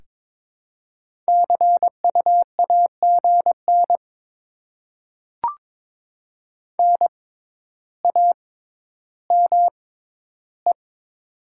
Morse code, heard as a keyed tone, about 22 wpm, spaced as for 12 wpm, 700 Hz. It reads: CUAGN NAME
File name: Top 20 QSO Elements - 6x Character Spacing - 22wpm